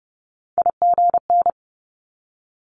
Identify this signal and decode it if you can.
Morse code, heard as a keyed tone, about 30 wpm, 700 Hz. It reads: IGN
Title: Multi-letter Phonemes 30wpm